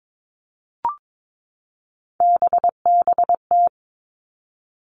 Morse code, heard as a keyed tone, about 22 wpm, 700 Hz. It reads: BBT
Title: Three Challenging Letter-Number Combinations 22wpm